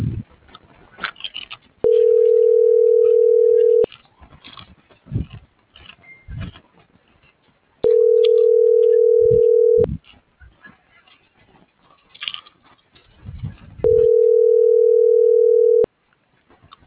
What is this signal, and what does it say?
Ringback tone (North America), 440 + 480 Hz.
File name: CENTRAL MABE / out luis.palacios - .30848